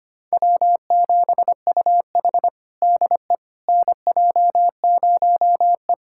Morse code, heard as a keyed tone, about 25 wpm, 700 Hz. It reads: W7UH DE NJ0E
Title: USA Callsign DE Callsign - 4 Characters - 25wpm